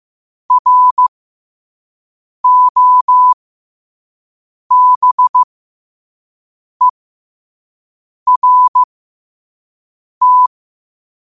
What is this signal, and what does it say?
Morse code, heard as a keyed tone, about 15 wpm, spaced as for 5 wpm, 1000 Hz. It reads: ROBERT
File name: qotdfiles / 5wpm